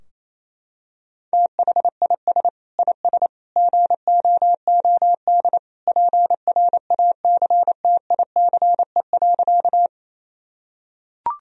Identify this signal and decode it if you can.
Morse code, heard as a keyed tone, about 28 wpm, 700 Hz. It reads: THIS IS GOOD PRACTICE.
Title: Sentences from Top 1000 Words - Review Part 4 - 28wpm